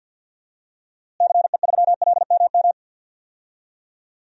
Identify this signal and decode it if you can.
Morse code, heard as a keyed tone, about 47 wpm, 700 Hz. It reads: CE3LMM
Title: DXCC-100 Call Signs - 6 Characters - Repeated 3x - 50wpm